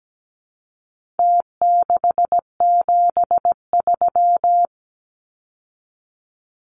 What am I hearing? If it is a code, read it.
Morse code, heard as a keyed tone, about 17 wpm, 700 Hz. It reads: T673
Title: Numbers - Contest Style - 4-Digit 17wpm